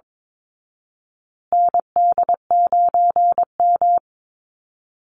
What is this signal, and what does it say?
Morse code, heard as a keyed tone, about 22 wpm, 700 Hz. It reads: ND9M
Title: Call Signs Repeated 2x - 22wpm